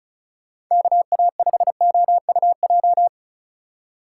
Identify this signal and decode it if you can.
Morse code, heard as a keyed tone, about 36 wpm, 700 Hz. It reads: KA5OUJ
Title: USA Call Signs - 6 Characters - 3x Character Spacing - 35wpm